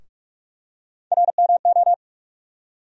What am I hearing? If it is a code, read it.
Morse code, heard as a keyed tone, about 48 wpm, 700 Hz. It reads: RMO
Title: USA Call Sign Postfixes - 3 Letters 45wpm